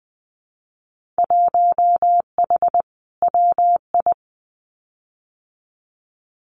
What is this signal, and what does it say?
Morse code, heard as a keyed tone, about 20 wpm, 700 Hz. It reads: 1H WI
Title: Winter Field Day Exchanges 20wpm